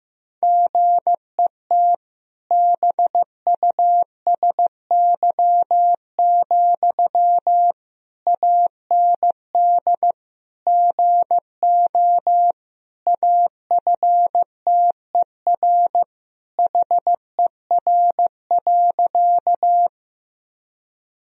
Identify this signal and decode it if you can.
Morse code, heard as a keyed tone, about 16 wpm, 700 Hz. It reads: GET BUSY, AND GO AFTER HER.
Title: Sentences from Top 1000 Words - Review Part 3 - 15wpm